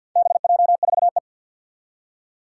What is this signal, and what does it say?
Morse code, heard as a keyed tone, about 49 wpm, 700 Hz. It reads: BJ4E
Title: Four Challenging Letter-Number Combinations 50wpm